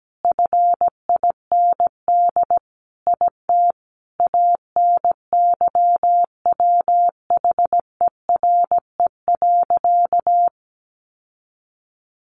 Morse code, heard as a keyed tone, about 17 wpm, 700 Hz. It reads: FIND IT ANYWHERE.
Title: Sentences from Top 1400 Words 17wpm